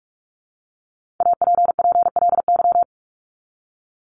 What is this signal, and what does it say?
Morse code, heard as a keyed tone, about 44 wpm, 700 Hz. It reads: APPLY
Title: ED Suffix - Top 100 Words with Base Words 45wpm